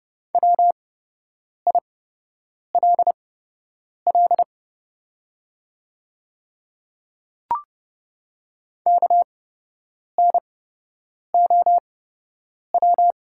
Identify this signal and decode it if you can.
Morse code, heard as a keyed tone, about 30 wpm, spaced as for 8 wpm, 700 Hz. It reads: WILL KNOW
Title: 8x Character Spacing - Top 100 Words - 30wpm